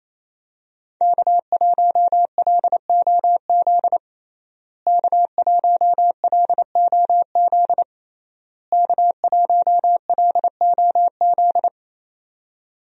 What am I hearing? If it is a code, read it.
Morse code, heard as a keyed tone, about 28 wpm, 700 Hz. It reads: K1LOZ K1LOZ K1LOZ
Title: USA Call Signs - 5 Characters - Repeated 3x - 28wpm